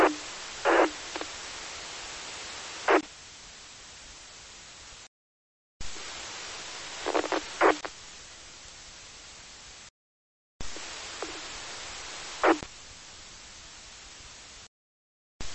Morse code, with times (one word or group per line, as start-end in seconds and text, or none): none